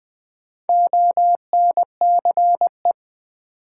Morse code, heard as a keyed tone, about 20 wpm, 700 Hz. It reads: ONCE
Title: Sentences from Top 300 Words - New Word Drill - 20wpm